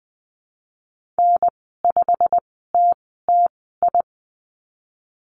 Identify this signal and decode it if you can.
Morse code, heard as a keyed tone, about 20 wpm, spaced as for 14 wpm, 700 Hz. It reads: N5TTI
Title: 088 Lesson 15g - Number 5 - Call Signs 20wpm